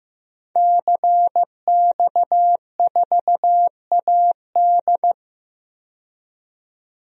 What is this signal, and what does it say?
Morse code, heard as a keyed tone, about 15 wpm, 700 Hz. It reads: CX4AD
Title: DXCC-100 Call Signs - 5 Characters - 15wpm